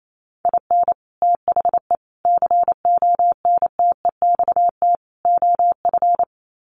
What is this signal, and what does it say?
Morse code, heard as a keyed tone, about 28 wpm, 700 Hz. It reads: IN THE CONTEXT OF
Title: Sets of 4 Words 28wpm